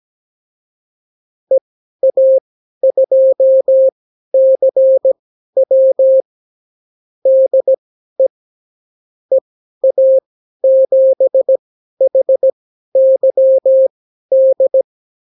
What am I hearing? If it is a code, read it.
Morse code, heard as a keyed tone, about 17 wpm, spaced as for 12 wpm, 530 Hz. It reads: EA2CW DE EA7HYD
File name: CW QSO libroCW4 18-8ppm